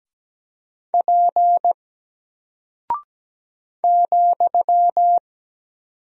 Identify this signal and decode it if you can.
Morse code, heard as a keyed tone, about 17 wpm, 700 Hz. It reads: P ,
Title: Single Character (Letter:Number:Punctuation:Prosign) 17wpm